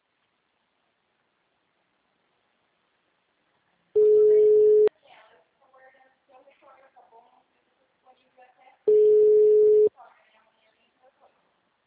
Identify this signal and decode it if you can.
Ringback tone (Europe, 425 Hz), with speech.
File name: out- -2026- - - .38069